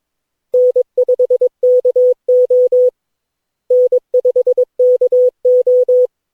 Morse code, heard as a keyed tone, about 22 wpm, 500 Hz. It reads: N5KO N5KO
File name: cw / Session9 cw training